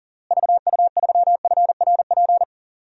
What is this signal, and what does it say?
Morse code, heard as a keyed tone, about 40 wpm, 700 Hz. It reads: VU3FRP
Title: DXCC-100 Call Signs - 6 Characters - Repeated 3x - 40wpm